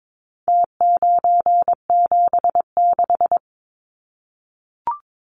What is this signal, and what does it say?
Morse code, heard as a keyed tone, about 22 wpm, 700 Hz. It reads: T976
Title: Numbers - Contest Style - 4-Digit 22wpm